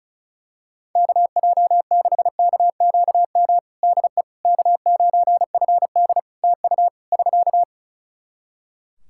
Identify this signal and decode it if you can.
Morse code, heard as a keyed tone, about 36 wpm, 700 Hz. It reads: KJ6KQM DE K9FD TU <SK>
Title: K1USN SST QSO Practice 35wpm